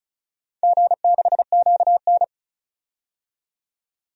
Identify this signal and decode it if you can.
Morse code, heard as a keyed tone, about 36 wpm, 700 Hz. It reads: G6QN
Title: DXCC-100 Call Signs - 4 Characters - 35wpm